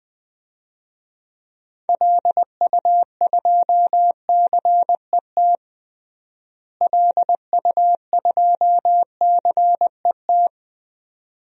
Morse code, heard as a keyed tone, about 20 wpm, 700 Hz. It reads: LU2CET LU2CET
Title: DXCC-100 Call Signs - 6 Characters - Repeated 2x - 20wpm